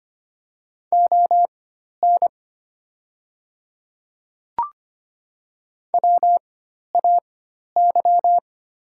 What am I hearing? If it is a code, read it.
Morse code, heard as a keyed tone, about 25 wpm, spaced as for 12 wpm, 700 Hz. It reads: ON WAY
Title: 4x Character Spacing - Top 200 Words - 25wpm